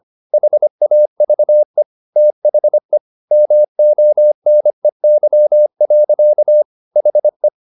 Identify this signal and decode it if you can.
Morse code, heard as a keyed tone, about 25 wpm, 600 Hz. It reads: HAVE THE MONEY. HE